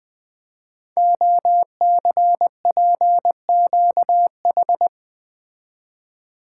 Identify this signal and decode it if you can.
Morse code, heard as a keyed tone, about 20 wpm, 700 Hz. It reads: OCPQH
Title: Code Groups - 5 Letter:Numbers - 20wpm